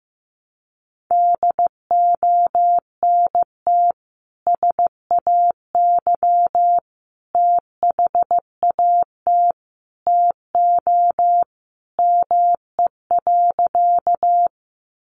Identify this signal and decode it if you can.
Morse code, heard as a keyed tone, about 15 wpm, 700 Hz. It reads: DONT SAY THAT TO ME.
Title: Sentences from Top 300 Words 15wpm